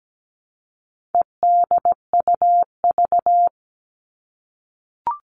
Morse code, heard as a keyed tone, about 17 wpm, 700 Hz. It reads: EDUV
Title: All Cut Numbers - 4-Digit 17wpm